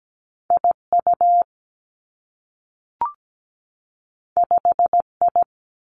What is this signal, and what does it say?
Morse code, heard as a keyed tone, about 17 wpm, 700 Hz. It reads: IU 5I
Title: Two Challenging Letter-Number Combinations 17wpm